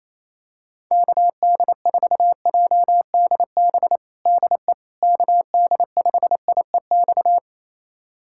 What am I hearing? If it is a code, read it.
Morse code, heard as a keyed tone, about 28 wpm, 700 Hz. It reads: KD4JDB DE KD5IEX
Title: USA Callsign DE Callsign 28wpm